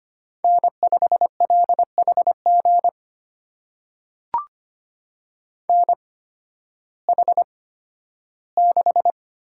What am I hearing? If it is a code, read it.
Morse code, heard as a keyed tone, about 25 wpm, spaced as for 13 wpm, 700 Hz. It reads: N5LHG NH6
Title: USA Call Signs - 5 Characters - 8x Character Spacing - 25wpm